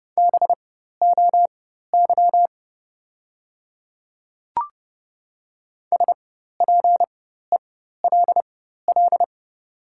Morse code, heard as a keyed tone, about 30 wpm, spaced as for 14 wpm, 700 Hz. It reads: BOY SPELL
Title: 4x Character Spacing - Top 200 Words - 30wpm